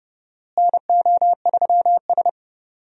Morse code, heard as a keyed tone, about 30 wpm, 700 Hz. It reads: NO3S
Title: USA Call Signs - 4 Characters - Repeated 3x - 30wpm